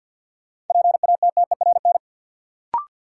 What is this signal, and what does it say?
Morse code, heard as a keyed tone, about 53 wpm, 700 Hz. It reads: PATTERN
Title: Sentences from Top 1000 Words - New Word Drill - 50wpm